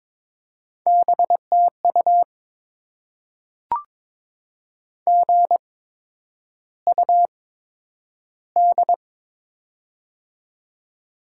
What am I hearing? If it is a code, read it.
Morse code, heard as a keyed tone, about 22 wpm, spaced as for 9 wpm, 700 Hz. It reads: BTU GUD
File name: Top 20 QSO Elements - 8x Character Spacing - 22wpm